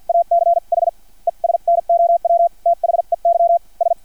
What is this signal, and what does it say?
Morse code, heard as a keyed tone, about 50 wpm, spaced as for 42 wpm, 680 Hz. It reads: ACH ES NOW THEY S
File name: archived / CRSnet